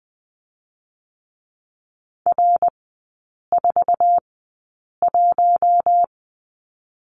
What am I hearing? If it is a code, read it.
Morse code, heard as a keyed tone, about 20 wpm, spaced as for 8 wpm, 690 Hz. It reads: R41
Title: 060 Lesson 11e - Letter h - Three Characters 20wpm